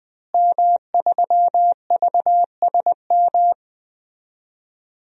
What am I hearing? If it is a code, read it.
Morse code, heard as a keyed tone, about 21 wpm, 700 Hz. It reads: M3VSM
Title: DXCC-100 Call Signs - 5 Characters - Repeated 3x - 20wpm